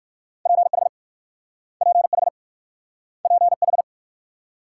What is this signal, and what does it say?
Morse code, heard as a keyed tone, about 56 wpm, 690 Hz. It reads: PH PH PH
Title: Speed-Racing Multi-letter Phonemes - 45wpm